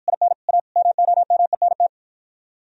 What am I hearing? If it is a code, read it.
Morse code, heard as a keyed tone, about 53 wpm, 700 Hz. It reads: IN A MOMENT